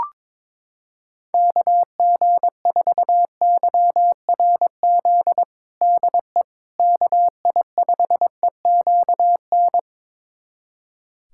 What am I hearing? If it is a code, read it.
Morse code, heard as a keyed tone, about 22 wpm, 700 Hz. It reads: KG4YRZ DE KI5EQN
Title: USA Callsign DE Callsign - 6 Characters - 22wpm